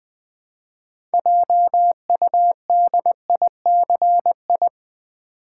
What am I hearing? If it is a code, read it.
Morse code, heard as a keyed tone, about 20 wpm, 700 Hz. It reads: JUDICI
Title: Build A Word - 3 Letters At A Time - 8-Letter Words - 20wpm